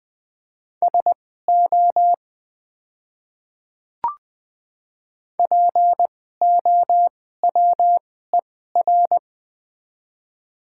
Morse code, heard as a keyed tone, about 20 wpm, spaced as for 15 wpm, 690 Hz. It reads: SO POWER
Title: 2x Character Spacing - Top 500 Words - 20wpm